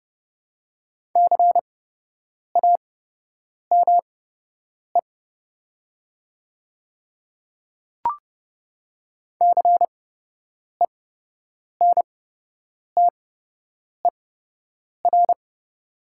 Morse code, heard as a keyed tone, about 30 wpm, spaced as for 8 wpm, 700 Hz. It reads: CAME CENTER
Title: 8x Character Spacing - Top 500 Words - 30wpm